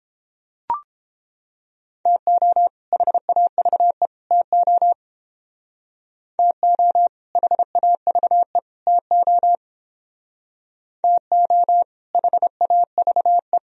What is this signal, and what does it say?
Morse code, heard as a keyed tone, about 31 wpm, 700 Hz. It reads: TO HAVE TO TO HAVE TO TO HAVE